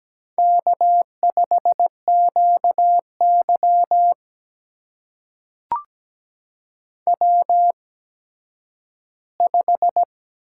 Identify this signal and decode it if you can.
Morse code, heard as a keyed tone, about 17 wpm, spaced as for 10 wpm, 700 Hz. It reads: K5QY W5
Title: USA Call Signs - 4 Characters - 8x Character Spacing - 17wpm